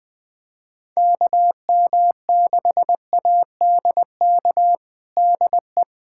Morse code, heard as a keyed tone, about 21 wpm, 700 Hz. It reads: KM6ADK DE
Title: USA Callsign DE Callsign - 6 Characters - 20wpm